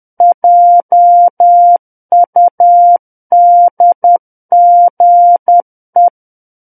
Morse code, heard as a keyed tone, about 10 wpm, 700 Hz. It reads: JUDGE
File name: Book One / 01-Chapter 1 - 10 WPM